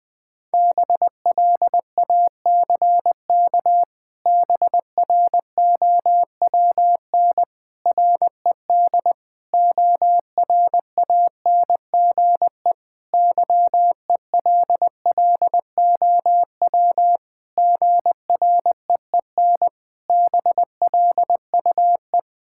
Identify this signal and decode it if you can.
Morse code, heard as a keyed tone, about 20 wpm, 700 Hz. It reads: BLACK BROWN RED ORANGE YELLOW GREEN BLUE
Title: Pangram Challenge v2 - 20wpm